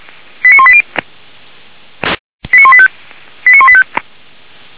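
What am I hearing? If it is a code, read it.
Five-tone selective call (ZVEI), heard as quick runs of five tones, tones of about 70 ms. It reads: 89180 89185 89185
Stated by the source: ZVEI five-tone call received on SDR.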